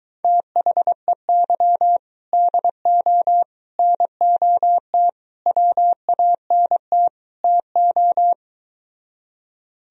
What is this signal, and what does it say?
Morse code, heard as a keyed tone, about 23 wpm, 700 Hz. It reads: THEY DO NOT WANT TO